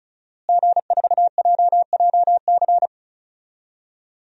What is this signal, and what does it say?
Morse code, heard as a keyed tone, about 34 wpm, 700 Hz. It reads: G4JJC
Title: DXCC-100 Call Signs - 5 Characters - Repeated 2x - 35wpm